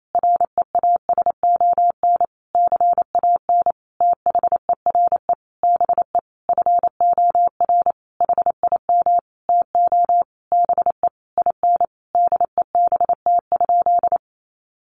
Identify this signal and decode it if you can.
Morse code, heard as a keyed tone, about 28 wpm, 700 Hz. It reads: REASON CAN THERE BE FOR HIM TO BE IN DEBT?
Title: Sentences from Top 1300 Words 28wpm